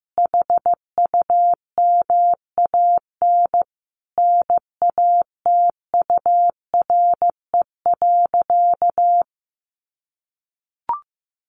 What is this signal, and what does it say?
Morse code, heard as a keyed tone, about 15 wpm, 700 Hz. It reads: HUMAN NATURE.
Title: Sentences from Top 700 Words 15wpm